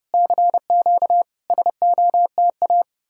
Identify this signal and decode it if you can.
Morse code, heard as a keyed tone, about 33 wpm, 700 Hz. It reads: CQ SOTA